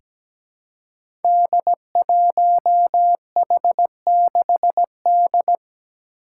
Morse code, heard as a keyed tone, about 17 wpm, 700 Hz. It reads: D1H6D
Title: Five Challenging Letter-Number Combinations 17wpm